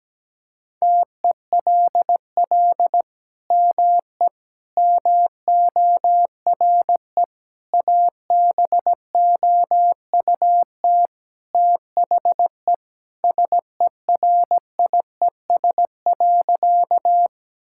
Morse code, heard as a keyed tone, about 17 wpm, 700 Hz. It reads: TELL ME MORE ABOUT THE SERIES.
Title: Sentences from Top 1000 Words - Review Part 5 - 17wpm